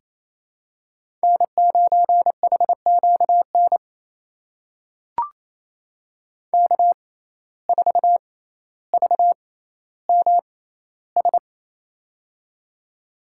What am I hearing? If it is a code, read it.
Morse code, heard as a keyed tone, about 28 wpm, spaced as for 14 wpm, 700 Hz. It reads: N9HQN K4VMS
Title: USA Call Signs - 5 Characters - 6x Character Spacing - 28wpm